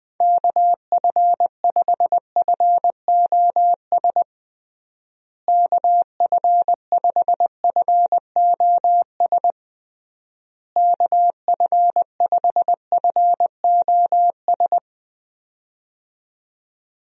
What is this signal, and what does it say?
Morse code, heard as a keyed tone, about 21 wpm, 700 Hz. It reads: KF5FOS KF5FOS KF5FOS
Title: USA Call Signs - 6 Characters - Repeated 3x - 20wpm